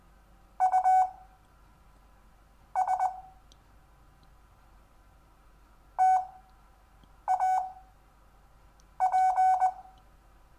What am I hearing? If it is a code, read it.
Morse code, heard as a keyed tone, about 20 wpm, spaced as for 6 wpm, 750 Hz. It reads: US TAP
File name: cw / Lesson4 REVIEW